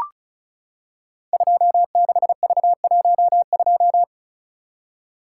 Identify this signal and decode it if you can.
Morse code, heard as a keyed tone, about 36 wpm, 700 Hz. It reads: 26V12
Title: Code Groups - 5 Letter:Numbers - 35wpm